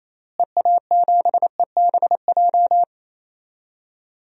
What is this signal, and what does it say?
Morse code, heard as a keyed tone, about 28 wpm, 700 Hz. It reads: EA7EBJ
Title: DXCC-100 Call Signs - 6 Characters - Repeated 2x - 28wpm